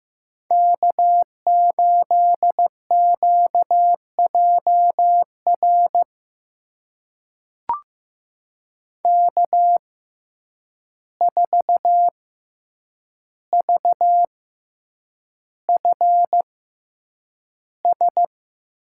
Morse code, heard as a keyed tone, about 15 wpm, spaced as for 8 wpm, 700 Hz. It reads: K8QJR K4VFS
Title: USA Call Signs - 5 Characters - 6x Character Spacing - 15wpm